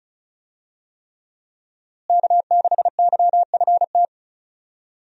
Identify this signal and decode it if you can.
Morse code, heard as a keyed tone, about 36 wpm, 700 Hz. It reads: K6YFT
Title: USA Call Signs - 5 Characters - Repeated 3x - 35wpm